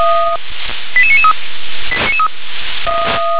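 Five-tone selective call (ZVEI), heard as quick runs of five tones, tones of about 70 ms. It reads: 88993
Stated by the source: ZVEI five-tone selective call received by SDR.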